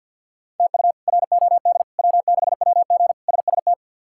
Morse code, heard as a keyed tone, about 49 wpm, 700 Hz. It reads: TU ROD W6WG SST
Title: K1USN SST QSO Practice - 50wpm